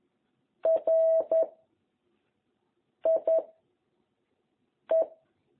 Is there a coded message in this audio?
Yes: Morse code from 0.7 to 5.0 s, heard as a keyed tone, about 11 wpm, 660 Hz. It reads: R I E